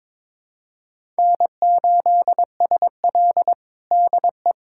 Morse code, heard as a keyed tone, about 22 wpm, 700 Hz. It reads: N8SL DE
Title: USA Callsign DE Callsign - 4 Characters - 22wpm